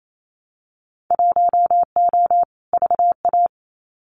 Morse code, heard as a keyed tone, about 28 wpm, 700 Hz. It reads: 1O VA
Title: Winter Field Day Exchanges 28wpm